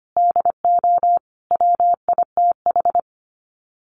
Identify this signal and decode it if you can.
Morse code, heard as a keyed tone, about 25 wpm, 700 Hz. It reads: DO WITH
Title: Sets of 5 Words 25wpm